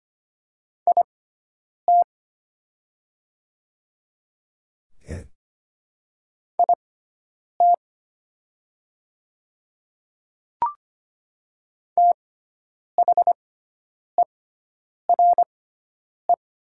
Morse code, heard as a keyed tone, about 25 wpm, spaced as for 9 wpm, 690 Hz. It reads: IT IT THERE
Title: 6x Character Spacing - Top 100 Words - 25wpm